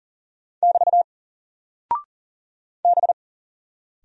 Morse code, heard as a keyed tone, about 40 wpm, 700 Hz. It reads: <BT> B
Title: Single Character (Letter:Number:Punctuation:Prosign) Alt 40wpm